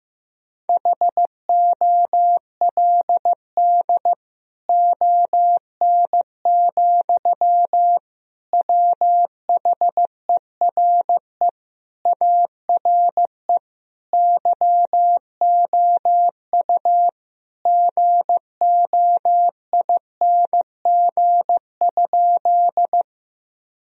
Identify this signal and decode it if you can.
Morse code, heard as a keyed tone, about 15 wpm, 700 Hz. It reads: HOLD ON, WHERE ARE YOU GOING?
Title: Sentences from Top 1000 Words - Review Part 2 - 15wpm